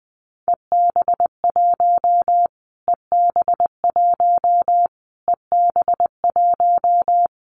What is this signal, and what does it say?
Morse code, heard as a keyed tone, about 20 wpm, 700 Hz. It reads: EB1 EB1 EB1
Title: Callsign Prefixes for Top 25 DXCC - Repeated 3x - 20wpm